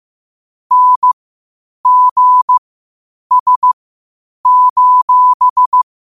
Morse code, heard as a keyed tone, about 15 wpm, spaced as for 8 wpm, 1000 Hz. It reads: NGS:
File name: qotdfiles / 8wpm